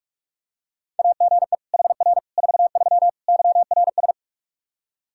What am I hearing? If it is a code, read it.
Morse code, heard as a keyed tone, about 46 wpm, 700 Hz. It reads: AGE HR 43 YRS
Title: K1USN SST QSO Practice 45wpm